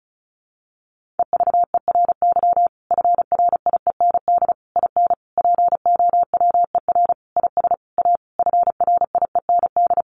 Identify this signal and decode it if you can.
Morse code, heard as a keyed tone, about 34 wpm, 700 Hz. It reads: EVERY FRIEND IN POWER IS A FRIEND
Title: Sentences from Top 1400 Words 35wpm